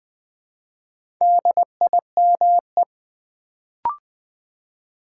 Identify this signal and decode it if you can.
Morse code, heard as a keyed tone, about 20 wpm, 700 Hz. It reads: DIME
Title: Build A Word - 2 Letters At A Time - 4-Letter Words - 20wpm